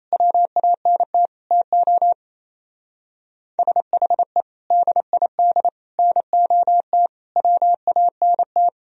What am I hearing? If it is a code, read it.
Morse code, heard as a keyed tone, about 35 wpm, 700 Hz. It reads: WANT TO SHE DID NOT WANT
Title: Speed-Racing - Sets of 5 Words 28wpm